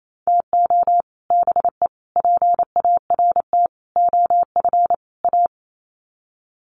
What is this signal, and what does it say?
Morse code, heard as a keyed tone, about 28 wpm, 700 Hz. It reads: TO BE PART OF A
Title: Sets of 5 Words Encore2 28wpm